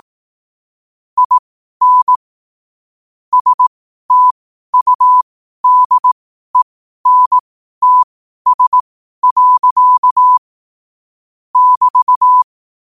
Morse code, heard as a keyed tone, about 18 wpm, spaced as for 12 wpm, 1000 Hz. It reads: IN STUDENTS. <BT>
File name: qotdfiles / 12wpm